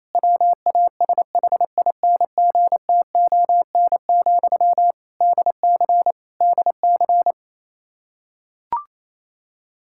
Morse code, heard as a keyed tone, about 28 wpm, 700 Hz. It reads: WASHINGTON, DC DC
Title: Practice QSO - Exchange 1 - Beginning - 28wpm